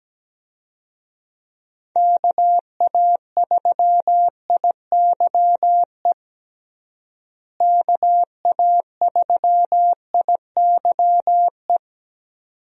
Morse code, heard as a keyed tone, about 17 wpm, 700 Hz. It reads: KA3IYE KA3IYE
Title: USA Call Signs - 6 Characters - Repeated 2x - 17wpm